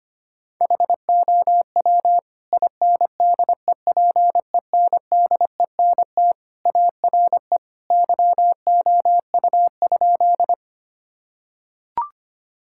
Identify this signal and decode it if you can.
Morse code, heard as a keyed tone, about 25 wpm, 700 Hz. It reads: HOW INDEPENDENT ARE YOU?
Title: Sentences from Top 1000 Words - Review Part 6 - 25wpm